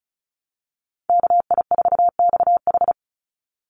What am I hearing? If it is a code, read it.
Morse code, heard as a keyed tone, about 34 wpm, 700 Hz. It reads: KI4XH
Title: Call Signs Repeated 3x - 35wpm